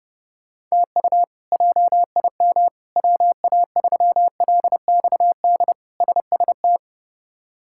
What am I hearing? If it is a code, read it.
Morse code, heard as a keyed tone, about 32 wpm, 700 Hz. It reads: TU JIM WA3LXD SST
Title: K1USN SST QSO Practice - 30wpm